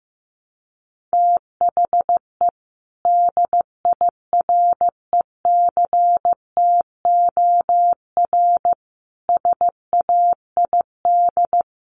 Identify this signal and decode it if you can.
Morse code, heard as a keyed tone, about 15 wpm, 700 Hz. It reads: THE DIRECTOR SAID
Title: Sentences from Top 1300 Words 15wpm